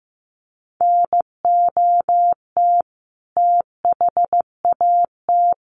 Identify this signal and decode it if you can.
Morse code, heard as a keyed tone, about 15 wpm, 700 Hz. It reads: NOT THAT
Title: A Sound Of Thunder 15wpm